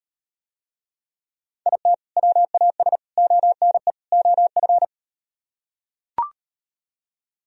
Morse code, heard as a keyed tone, about 37 wpm, 700 Hz. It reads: IT WAS ONE OF